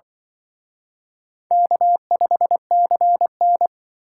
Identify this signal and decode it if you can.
Morse code, heard as a keyed tone, about 25 wpm, 700 Hz. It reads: K5CN
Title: Speed Racing - USA 1x2 Call Signs 20wpm